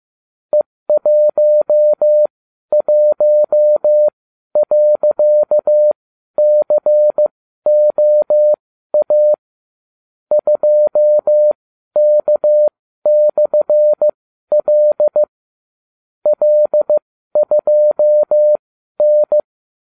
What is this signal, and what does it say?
Morse code, heard as a keyed tone, about 15 wpm, spaced as for 12 wpm, 600 Hz. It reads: E11.COA 2K/L L2N